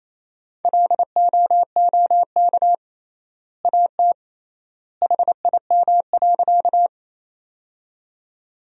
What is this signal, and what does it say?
Morse code, heard as a keyed tone, about 28 wpm, 700 Hz. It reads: LOOK AT HIM.
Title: 3x Word-Spacing - Sentences from Top 100 Words - 28wpm